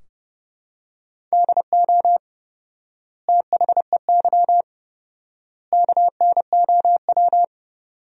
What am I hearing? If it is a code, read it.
Morse code, heard as a keyed tone, about 30 wpm, 700 Hz. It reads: DO THEY KNOW
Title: 4x Word-Spacing - Sentences from Top 100 Words - 30wpm